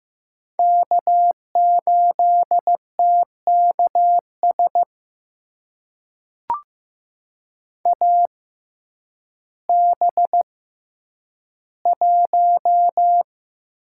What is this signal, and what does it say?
Morse code, heard as a keyed tone, about 15 wpm, spaced as for 9 wpm, 700 Hz. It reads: K8TKS AB1
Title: USA Call Signs - 5 Characters - 6x Character Spacing - 15wpm